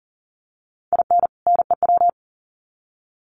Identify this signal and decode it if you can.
Morse code, heard as a keyed tone, about 40 wpm, 700 Hz. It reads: IN NEW
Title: Sets of 2 Words 40wpm